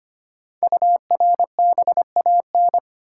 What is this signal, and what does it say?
Morse code, heard as a keyed tone, about 26 wpm, 700 Hz. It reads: URBAN
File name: 5 Letter Words - First 3 Letters Given - 25wpm